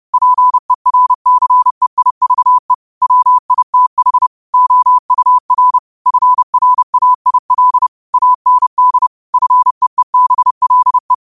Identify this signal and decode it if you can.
Morse code, heard as a keyed tone, about 30 wpm, 1000 Hz. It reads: PERCEIVE WITH OUR FRAIL AND FEEBLE